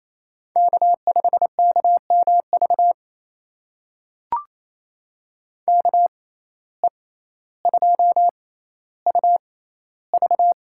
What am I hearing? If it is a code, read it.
Morse code, heard as a keyed tone, about 28 wpm, spaced as for 14 wpm, 700 Hz. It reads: K5KMV KE2UV
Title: USA Call Signs - 5 Characters - 6x Character Spacing - 28wpm